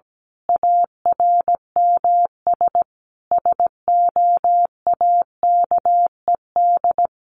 Morse code, heard as a keyed tone, about 17 wpm, 700 Hz. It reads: ARMS SOAKED
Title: A Sound Of Thunder 17wpm